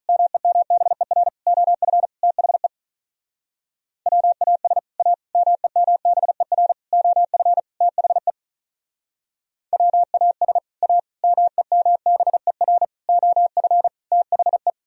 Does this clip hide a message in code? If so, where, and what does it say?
Yes: Morse code from 0.1 to 14.7 s, heard as a keyed tone, about 46 wpm, 700 Hz. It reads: MEMBER OF THE WAS A MEMBER OF THE WAS A MEMBER OF THE